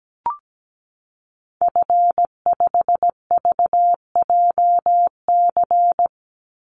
Morse code, heard as a keyed tone, about 17 wpm, 700 Hz. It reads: F5VJC
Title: Call Signs Repeated 2x - 17wpm